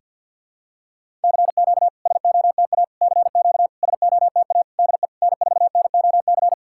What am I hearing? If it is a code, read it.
Morse code, heard as a keyed tone, about 48 wpm, 700 Hz. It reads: CQ SOTA CQ SOTA DE N4NOC
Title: SOTA QSO Practice 50wpm